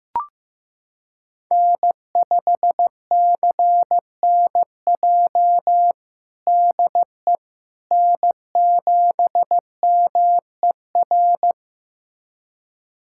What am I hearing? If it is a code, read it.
Morse code, heard as a keyed tone, about 15 wpm, 700 Hz. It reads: N5CNJ DE N7MER
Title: USA Callsign DE Callsign - 5 Characters - 15wpm